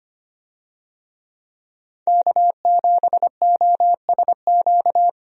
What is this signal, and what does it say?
Morse code, heard as a keyed tone, about 25 wpm, 700 Hz. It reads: K7OSQ
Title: USA Call Signs - 5 Characters - 25wpm